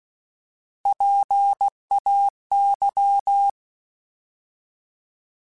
Morse code, heard as a keyed tone, about 16 wpm, 800 Hz. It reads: PAY
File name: Alphabetical / Alphabetical mp3s / Ps @15